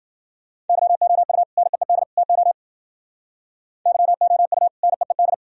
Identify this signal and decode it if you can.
Morse code, heard as a keyed tone, about 51 wpm, 700 Hz. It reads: YOU NEED TO YOU NEED